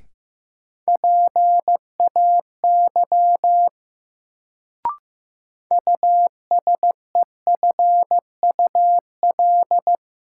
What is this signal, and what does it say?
Morse code, heard as a keyed tone, about 16 wpm, 700 Hz. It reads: PAY USEFUL
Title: Sentences from Top 700 Words - New Word Drill - 15wpm